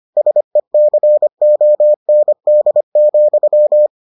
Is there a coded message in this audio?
Yes: Morse code from 0.2 to 3.9 s, heard as a keyed tone, about 25 wpm, 600 Hz. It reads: SECOND,